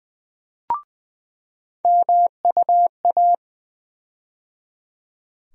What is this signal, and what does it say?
Morse code, heard as a keyed tone, about 20 wpm, 700 Hz. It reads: MUA
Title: USA Call Sign Postfixes - 3 Letters 20wpm